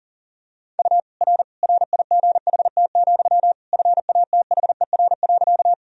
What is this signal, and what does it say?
Morse code, heard as a keyed tone, about 40 wpm, 700 Hz. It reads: U R RIGHT, FATHER.
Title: Sentences from Top 300 Words and Abbreviations 40wpm